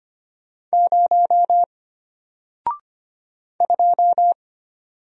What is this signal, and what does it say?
Morse code, heard as a keyed tone, about 25 wpm, 700 Hz. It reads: 0 2
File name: Single-Letter-Number-25wpm